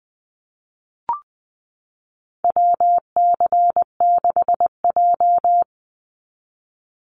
Morse code, heard as a keyed tone, about 20 wpm, 700 Hz. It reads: WC6J
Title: NA SOTA Chaser Callsigns 20wpm